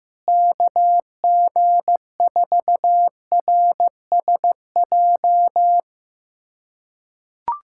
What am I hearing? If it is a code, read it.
Morse code, heard as a keyed tone, about 15 wpm, 700 Hz. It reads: KG4RSJ
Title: USA Callsign DE Callsign - 6 Characters - 15wpm